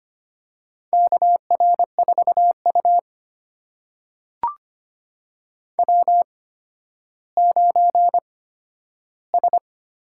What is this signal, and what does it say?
Morse code, heard as a keyed tone, about 25 wpm, spaced as for 12 wpm, 700 Hz. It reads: KR4U W9S
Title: USA Call Signs - 4 Characters - 8x Character Spacing - 25wpm